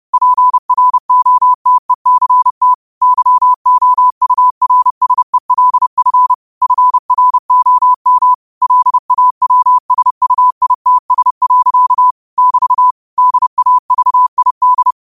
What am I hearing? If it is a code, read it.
Morse code, heard as a keyed tone, about 30 wpm, 1000 Hz. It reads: PROTECT YOURSELF FROM LAWSUITS. <BT> DAVID
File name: qotdfiles / 30wpm